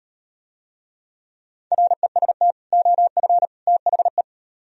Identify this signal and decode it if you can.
Morse code, heard as a keyed tone, about 36 wpm, 700 Hz. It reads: REST OF THE